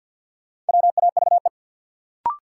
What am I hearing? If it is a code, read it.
Morse code, heard as a keyed tone, about 50 wpm, 700 Hz. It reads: WAVE
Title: Sentences from Top 1400 Words - New Word Drill - 50wpm